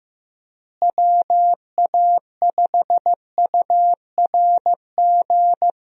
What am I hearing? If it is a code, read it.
Morse code, heard as a keyed tone, about 15 wpm, 700 Hz. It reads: WA5URG
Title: USA Call Signs - 6 Characters - Repeated 2x - 15wpm